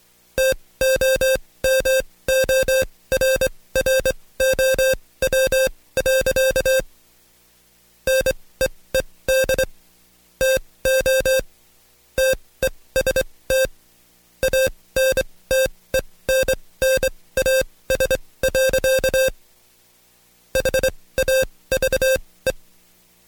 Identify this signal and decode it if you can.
Morse code, heard as a keyed tone, about 23 wpm, spaced as for 17 wpm, 530 Hz. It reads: TOMORROW. NEED TO TEST ANTENNAS. HAVE
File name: Code Quick / AudioDownld2 / Seg135-Stepper8 17WPM